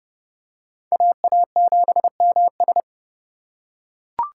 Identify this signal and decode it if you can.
Morse code, heard as a keyed tone, about 28 wpm, 700 Hz. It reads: AA7MS